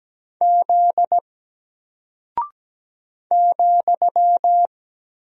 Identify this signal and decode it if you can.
Morse code, heard as a keyed tone, about 17 wpm, 700 Hz. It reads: Z ,
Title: Single Character (Letter:Number:Punctuation:Prosign) Alt 17wpm